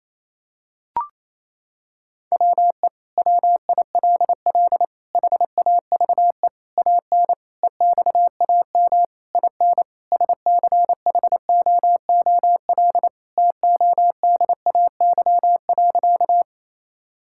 Morse code, heard as a keyed tone, about 27 wpm, 700 Hz. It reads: WE WILL HAVE AN EXAM IN SCHOOL TODAY.
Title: Sentences from Top 1000 Words - Review Part 5 - 28wpm